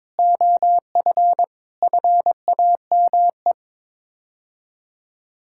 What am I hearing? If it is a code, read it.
Morse code, heard as a keyed tone, about 22 wpm, 700 Hz. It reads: OF FAME
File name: practice / 22wpm / Sets of 6 Words - 22wpm